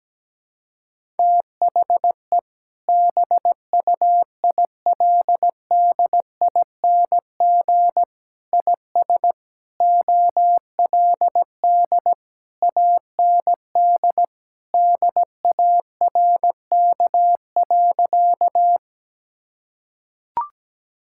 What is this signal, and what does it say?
Morse code, heard as a keyed tone, about 17 wpm, 700 Hz. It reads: THE BUILDING IS OLD AND DARK.
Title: Sentences from Top 1000 Words - Review Part 5 - 17wpm